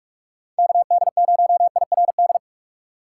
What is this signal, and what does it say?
Morse code, heard as a keyed tone, about 59 wpm, 700 Hz. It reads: KD0IRD